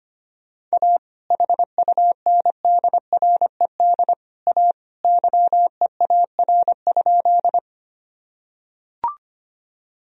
Morse code, heard as a keyed tone, about 25 wpm, 700 Hz. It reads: A HUNDRED A YEAR?
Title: Sentences from Top 1000 Words - Review Part 2 - 25wpm